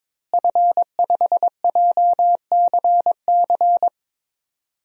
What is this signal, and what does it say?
Morse code, heard as a keyed tone, about 21 wpm, 700 Hz. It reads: F5JCC
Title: DXCC-100 Call Signs - 5 Characters - Repeated 3x - 22wpm